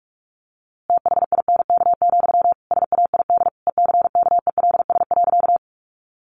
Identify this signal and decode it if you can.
Morse code, heard as a keyed tone, about 44 wpm, 700 Hz. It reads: THINK, SAID ECKELS.
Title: A Sound Of Thunder 45wpm